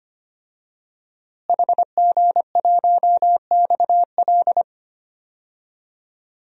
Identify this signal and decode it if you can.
Morse code, heard as a keyed tone, about 25 wpm, 700 Hz. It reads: HG1XL
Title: DXCC-100 Call Signs - 5 Characters - Repeated 2x - 25wpm